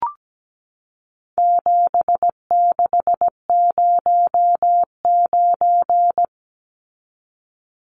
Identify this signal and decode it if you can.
Morse code, heard as a keyed tone, about 17 wpm, 700 Hz. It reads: 7609
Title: Numbers - 4-Digit 17wpm